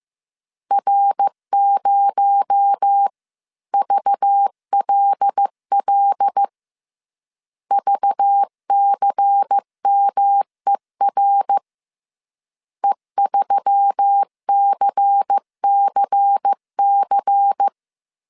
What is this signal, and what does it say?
Morse code, heard as a keyed tone, about 15 wpm, 800 Hz. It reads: R0VLL VCMER E3CCC